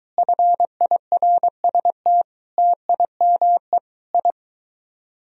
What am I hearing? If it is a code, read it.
Morse code, heard as a keyed tone, about 23 wpm, 700 Hz. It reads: FIRST TIME I